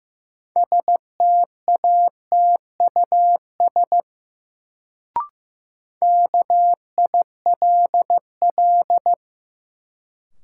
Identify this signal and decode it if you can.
Morse code, heard as a keyed tone, about 15 wpm, 700 Hz. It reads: STATUS KILL
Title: Sentences from Top 1400 Words - New Word Drill Alt - 15wpm